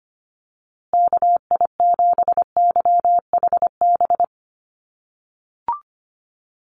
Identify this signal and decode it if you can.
Morse code, heard as a keyed tone, about 25 wpm, 700 Hz. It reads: KI7YHB
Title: USA Call Signs Repeated 3x 25wpm